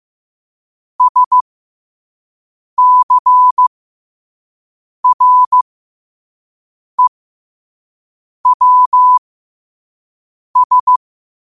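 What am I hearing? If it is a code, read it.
Morse code, heard as a keyed tone, about 15 wpm, spaced as for 5 wpm, 1000 Hz. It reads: SCREWS
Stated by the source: Morse code quote of the day sent at 5 wpm.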